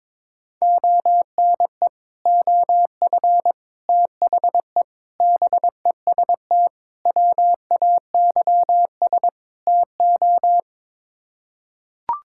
Morse code, heard as a keyed tone, about 22 wpm, 700 Hz. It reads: ONE OF THE BEST WAYS TO
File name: practice / 22wpm / Sets of 6 Words - 22wpm